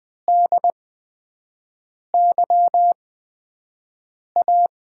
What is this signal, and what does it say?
Morse code, heard as a keyed tone, about 20 wpm, spaced as for 5 wpm, 700 Hz. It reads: DYA
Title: USA Call Signs - 6 Characters - 8x Character Spacing - 20wpm